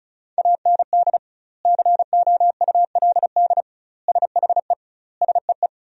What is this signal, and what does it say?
Morse code, heard as a keyed tone, about 34 wpm, 700 Hz. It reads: AND COULD SHE SEE
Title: 2x Word-Spacing - Sentences from Top 100 Words - 35wpm